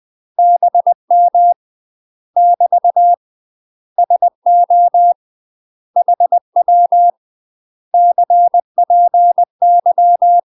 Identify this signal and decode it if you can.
Morse code, heard as a keyed tone, about 20 wpm, 700 Hz. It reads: BM <BT> SO HW CPY
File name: UK MASTER QSOs TRAINING / 20WPM / UK MASTER QSOs 0027